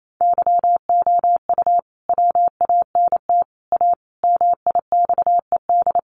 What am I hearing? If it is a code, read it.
Morse code, heard as a keyed tone, about 28 wpm, 700 Hz. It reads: YOU WANT A MIXED
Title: Sentences from Top 1500 Words 28wpm